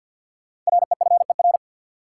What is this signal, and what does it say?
Morse code, heard as a keyed tone, about 52 wpm, 700 Hz. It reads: REFER
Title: Sentences from Top 1000 Words - New Word Drill - 50wpm